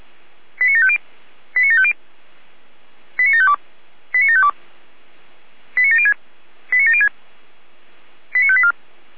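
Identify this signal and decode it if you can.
Five-tone selective call (ZVEI), heard as quick runs of five tones, tones of about 70 ms. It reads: 89750 89750 89752 89752 89706 89706 89584
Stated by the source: Five-tone ZVEI selective call received on an SDR.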